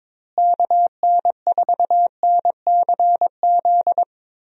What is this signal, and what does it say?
Morse code, heard as a keyed tone, about 22 wpm, 700 Hz. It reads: KN4NCZ
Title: USA Call Signs - 6 Characters - Repeated 3x - 22wpm